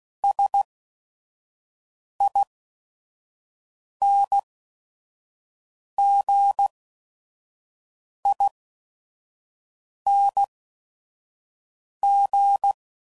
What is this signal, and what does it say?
Morse code, heard as a keyed tone, about 16 wpm, spaced as for 5 wpm, 800 Hz. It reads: SINGING